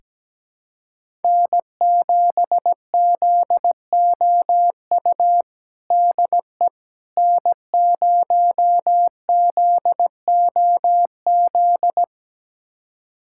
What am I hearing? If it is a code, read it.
Morse code, heard as a keyed tone, about 17 wpm, 700 Hz. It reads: N7ZOU DE N0ZOZ
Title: USA Callsign DE Callsign - 5 Characters - 17wpm